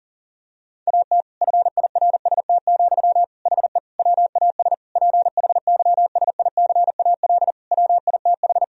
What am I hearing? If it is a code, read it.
Morse code, heard as a keyed tone, about 40 wpm, 700 Hz. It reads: AT FIRST, HE WAS PHYSICAL WITH
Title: Sentences from Top 1000 Words - Review Part 4 - 40wpm